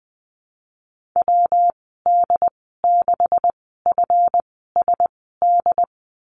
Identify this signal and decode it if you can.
Morse code, heard as a keyed tone, about 20 wpm, spaced as for 15 wpm, 700 Hz. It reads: WD6FSD
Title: 179 Lesson 29g - Slash - Call Signs 20wpm